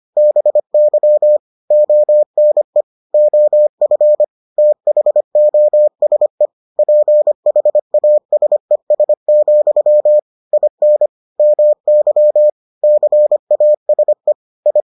Morse code, heard as a keyed tone, about 25 wpm, 600 Hz. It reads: BY ONE OF THOSE PHASES, IN MY CASE I